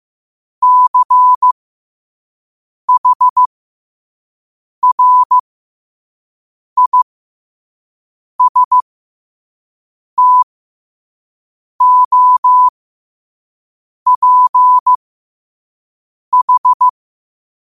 Morse code, heard as a keyed tone, about 15 wpm, spaced as for 5 wpm, 1000 Hz. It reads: CHRISTOPH
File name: qotdfiles / 5wpm